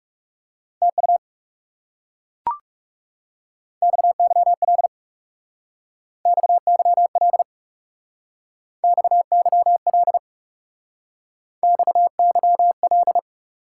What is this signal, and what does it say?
Morse code, heard as a keyed tone, about 35 wpm, 700 Hz. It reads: TU XYL XYL XYL XYL
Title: Speed-Racing Top 90 QSO Elements 30wpm